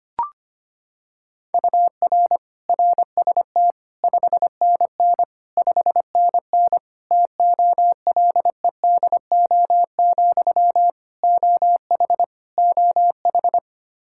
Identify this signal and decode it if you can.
Morse code, heard as a keyed tone, about 25 wpm, 700 Hz. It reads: UR RST 5NN 5NN TOLEDO, OH OH
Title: Practice QSO - Exchange 1 - Beginning - 25wpm